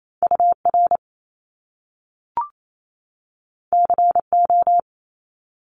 Morse code, heard as a keyed tone, about 28 wpm, 700 Hz. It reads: UR CO
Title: Top Two Letter Combinations in Words 28wpm